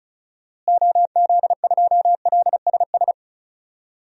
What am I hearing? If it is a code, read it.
Morse code, heard as a keyed tone, about 36 wpm, 700 Hz. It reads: OZ2LSS
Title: DXCC-100 Call Signs - 6 Characters - Repeated 3x - 35wpm